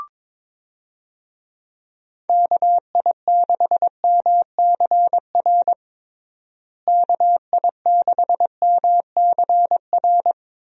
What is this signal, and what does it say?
Morse code, heard as a keyed tone, about 22 wpm, 700 Hz. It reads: KI6MCR KI6MCR
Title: USA Call Signs - 6 Characters - Repeated 2x - 22wpm